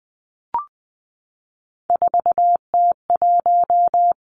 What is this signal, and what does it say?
Morse code, heard as a keyed tone, about 20 wpm, 700 Hz. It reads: 4T1
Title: Three Challenging Letter-Number Combinations 20wpm